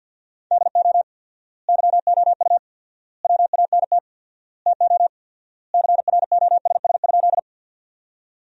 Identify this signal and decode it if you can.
Morse code, heard as a keyed tone, about 53 wpm, 700 Hz. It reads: DO YOU WANT TO CROSS?
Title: 4x Word-Spacing - Sentences from Top 300 Words - 50wpm